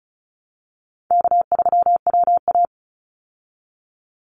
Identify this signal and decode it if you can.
Morse code, heard as a keyed tone, about 34 wpm, 700 Hz. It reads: K3WA
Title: Easy Call Signs Repeated 3x - 35wpm